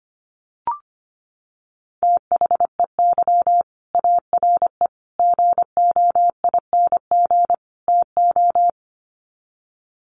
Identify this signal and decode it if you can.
Morse code, heard as a keyed tone, about 25 wpm, 700 Hz. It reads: THEY ARE GOING TO
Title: Sets of 4 Words 25wpm